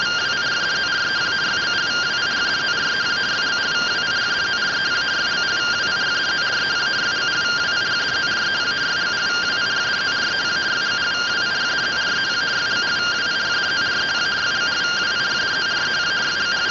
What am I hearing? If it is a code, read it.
Two-tone teleprinter signal, about 170 Hz shift.